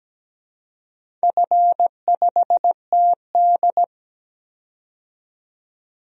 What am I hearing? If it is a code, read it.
Morse code, heard as a keyed tone, about 17 wpm, 700 Hz. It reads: F5TD
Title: DXCC-100 Call Signs - 4 Characters - 17wpm